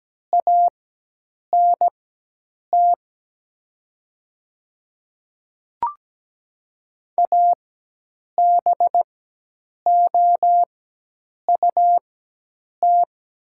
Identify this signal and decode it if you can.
Morse code, heard as a keyed tone, about 17 wpm, spaced as for 8 wpm, 700 Hz. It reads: ANT ABOUT
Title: 4x Character Spacing - Top 200 Words - 17wpm